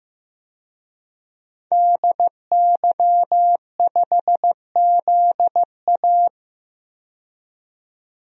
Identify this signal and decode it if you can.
Morse code, heard as a keyed tone, about 15 wpm, 700 Hz. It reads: DY5ZA
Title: DXCC-100 Call Signs - 5 Characters - 15wpm